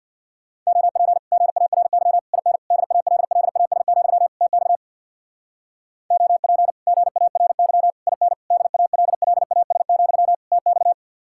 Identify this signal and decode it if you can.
Morse code, heard as a keyed tone, about 67 wpm, 700 Hz. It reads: OP GARY IN DALLAS, TX OP GARY IN DALLAS, TX